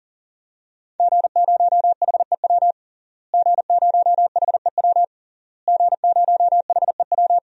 Morse code, heard as a keyed tone, about 40 wpm, 700 Hz. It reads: G0HEW G0HEW G0HEW
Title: DXCC-100 Call Signs - 5 Characters - Repeated 3x - 40wpm